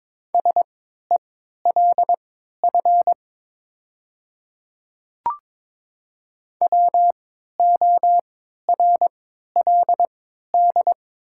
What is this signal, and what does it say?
Morse code, heard as a keyed tone, about 22 wpm, spaced as for 12 wpm, 700 Hz. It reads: SELF WORLD
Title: 3x Character Spacing - Top 200 Words - 22wpm